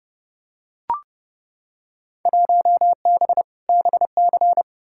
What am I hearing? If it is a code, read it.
Morse code, heard as a keyed tone, about 29 wpm, 700 Hz. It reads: 1B BC
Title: Field Day Exchanges - 30wpm